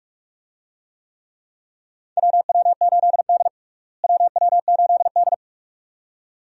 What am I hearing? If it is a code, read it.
Morse code, heard as a keyed tone, about 41 wpm, 700 Hz. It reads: WW8D WW8D
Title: USA Call Signs - 4 Characters - Repeated 2x - 45wpm